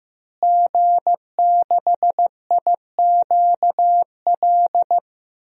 Morse code, heard as a keyed tone, about 15 wpm, 700 Hz. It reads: G6IQL
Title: DXCC-100 Call Signs - 5 Characters - Repeated 2x - 15wpm